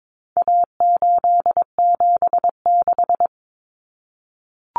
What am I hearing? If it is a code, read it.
Morse code, heard as a keyed tone, about 22 wpm, 700 Hz. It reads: A876
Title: Common Cut Numbers - 4-Digit 22wpm